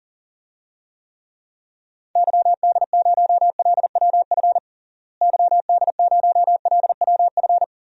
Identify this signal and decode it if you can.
Morse code, heard as a keyed tone, about 40 wpm, 700 Hz. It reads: YD0LWF YD0LWF
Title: DXCC-100 Call Signs - 6 Characters - Repeated 3x - 40wpm